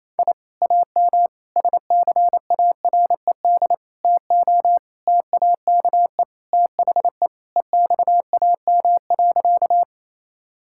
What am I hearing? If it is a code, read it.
Morse code, heard as a keyed tone, about 27 wpm, 700 Hz. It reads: I AM SCARED TO TAKE THE EXAM.
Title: Sentences from Top 1000 Words - Review Part 5 - 28wpm